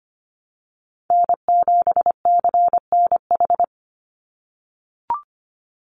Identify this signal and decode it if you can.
Morse code, heard as a keyed tone, about 25 wpm, 700 Hz. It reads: N7CNH
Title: NA SOTA Chaser Callsigns 25wpm